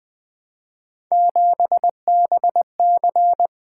Morse code, heard as a keyed tone, about 20 wpm, 700 Hz. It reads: 7BC
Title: Three Letter-Number Combinations - 20wpm